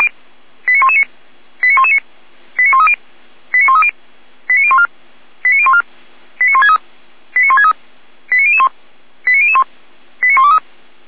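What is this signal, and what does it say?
Five-tone selective call (ZVEI), heard as quick runs of five tones, tones of about 80 ms. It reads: 89119 89119 89130 89130 89014 89014 89163 89163 89001 89001 89123
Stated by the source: ZVEI five-tone call 89017 received via SDR.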